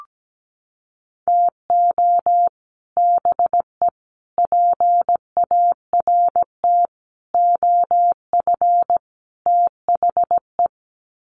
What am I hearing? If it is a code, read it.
Morse code, heard as a keyed tone, about 17 wpm, 700 Hz. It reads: TO BE PART OF THE
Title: Sets of 5 Words 17wpm